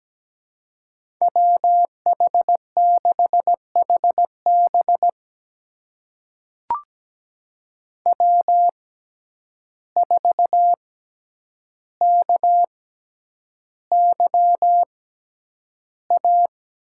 Morse code, heard as a keyed tone, about 17 wpm, spaced as for 9 wpm, 700 Hz. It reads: WH6HB W4KYA
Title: USA Call Signs - 5 Characters - 6x Character Spacing - 17wpm